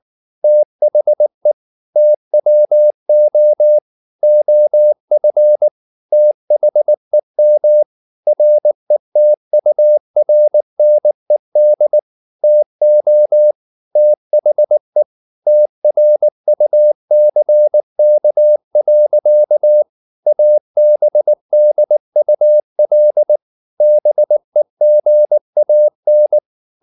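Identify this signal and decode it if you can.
Morse code, heard as a keyed tone, about 19 wpm, 600 Hz. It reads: THE TWO OF THEM RETURNED TO THE TRUCK. ABDUL BEGAN
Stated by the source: Morse code at 19 wpm from CW med Gnister, reading Ken Follett's novel Never in English.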